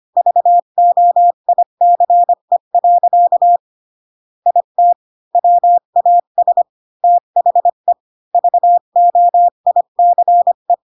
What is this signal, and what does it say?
Morse code, heard as a keyed tone, about 25 wpm, 700 Hz. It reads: VOICE. IT WAS THE VOICE